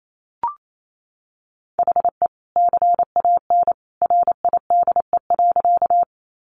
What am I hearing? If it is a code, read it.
Morse code, heard as a keyed tone, about 28 wpm, 700 Hz. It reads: HE CAN RIDE.
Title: Sentences from Top 1200 Words 28wpm